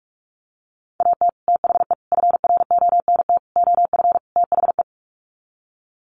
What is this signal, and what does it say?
Morse code, heard as a keyed tone, about 44 wpm, 700 Hz. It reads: AT THE FRONT OF THE
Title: Sets of 5 Words 45wpm